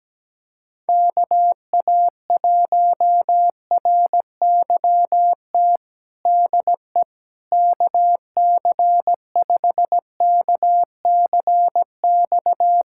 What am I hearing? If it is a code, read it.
Morse code, heard as a keyed tone, about 17 wpm, 700 Hz. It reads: KA1RYT DE KC5KCX
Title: USA Callsign DE Callsign - 6 Characters - 17wpm